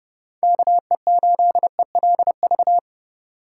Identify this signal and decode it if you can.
Morse code, heard as a keyed tone, about 30 wpm, 700 Hz. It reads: KE8ELV